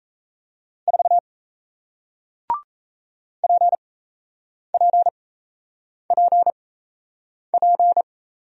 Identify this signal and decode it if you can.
Morse code, heard as a keyed tone, about 38 wpm, spaced as for 8 wpm, 700 Hz. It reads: 4 PPPP